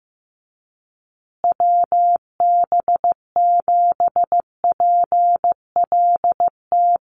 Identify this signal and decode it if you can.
Morse code, heard as a keyed tone, about 15 wpm, 700 Hz. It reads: WB7PLT
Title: USA Call Signs Repeated 3x 15wpm